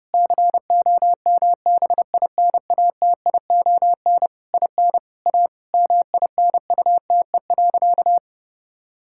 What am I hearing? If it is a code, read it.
Morse code, heard as a keyed tone, about 30 wpm, 700 Hz. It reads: COMBINATION IN A MINUTE.
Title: Sentences from Top 1000 Words - Review Part 6 - 30wpm